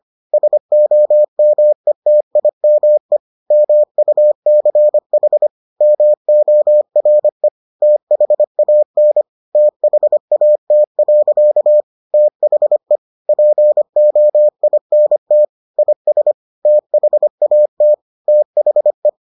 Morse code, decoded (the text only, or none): SOMETIME MUCH MORE THAN THAT. THE POINT IS THAT THE